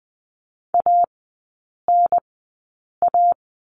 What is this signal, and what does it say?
Morse code, heard as a keyed tone, about 20 wpm, spaced as for 8 wpm, 700 Hz. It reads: ANA
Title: 172 Lesson 28e - Number 9 - Three Characters 20wpm